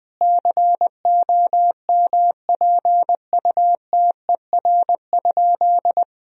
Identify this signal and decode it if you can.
Morse code, heard as a keyed tone, about 20 wpm, 700 Hz. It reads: COMPUTER?
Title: Sentences from Top 1000 Words - Review Part 3 - 20wpm